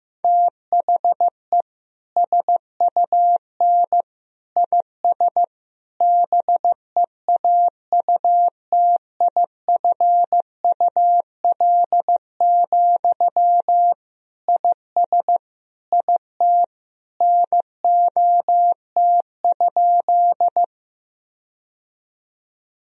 Morse code, heard as a keyed tone, about 16 wpm, 700 Hz. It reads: THE SUN IS BEAUTIFUL, IS IT NOT?
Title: Sentences from Top 1000 Words - Review Part 1 - 15wpm